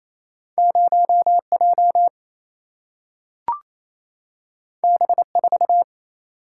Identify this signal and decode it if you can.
Morse code, heard as a keyed tone, about 28 wpm, 700 Hz. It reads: 0J B4
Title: Single-Letter Single-Number 28wpm